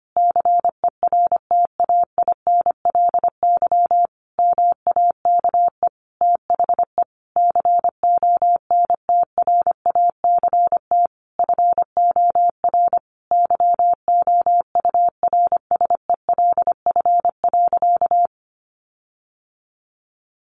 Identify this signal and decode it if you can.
Morse code, heard as a keyed tone, about 25 wpm, 700 Hz. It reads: CERTAINLY MAKE THE CONTRACT FOR YOURSELF.
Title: Sentences from Top 1400 Words 25wpm